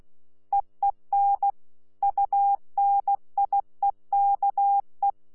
Morse code, heard as a keyed tone, about 16 wpm, 800 Hz. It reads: EEN UNIEKE
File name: Z PI4VRZ / PI4VRZ 14-jun-2025